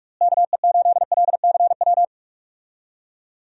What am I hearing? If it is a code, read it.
Morse code, heard as a keyed tone, about 46 wpm, 700 Hz. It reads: KE8LCW
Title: USA Call Signs - 6 Characters - Repeated 2x - 45wpm